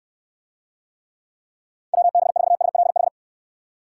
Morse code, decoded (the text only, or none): WB4SLH